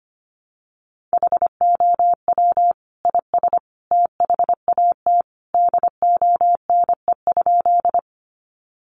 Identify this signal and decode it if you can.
Morse code, heard as a keyed tone, about 25 wpm, 700 Hz. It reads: HOW IS THAT DONE?
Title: Sentences from Top 500 Words 25wpm